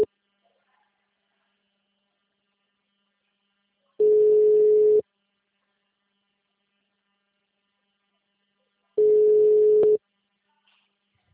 Ringback tone (Europe, 425 Hz).
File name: out- -2006- - - .65376